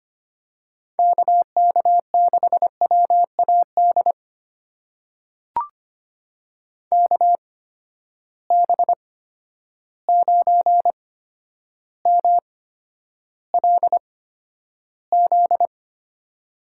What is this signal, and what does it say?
Morse code, heard as a keyed tone, about 25 wpm, spaced as for 11 wpm, 700 Hz. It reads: KK6WAD KB9MLZ
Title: USA Call Signs - 8x Character Spacing - 25wpm